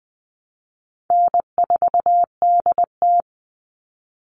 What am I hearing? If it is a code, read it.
Morse code, heard as a keyed tone, about 20 wpm, 700 Hz. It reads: N4DT
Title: Call Signs Repeated 2x - 20wpm